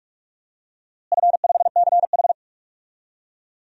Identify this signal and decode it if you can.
Morse code, heard as a keyed tone, about 46 wpm, 700 Hz. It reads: F5CH
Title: DXCC-100 Call Signs - 4 Characters - Repeated 3x - 45wpm